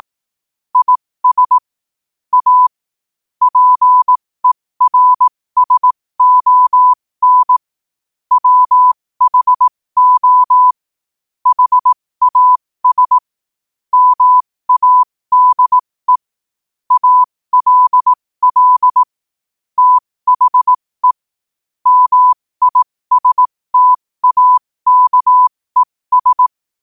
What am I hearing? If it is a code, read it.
Morse code, heard as a keyed tone, about 18 wpm, spaced as for 15 wpm, 1000 Hz. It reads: IS A PERSON WHO HAS MADE ALL THE MISTAKES
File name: qotdfiles / 15wpm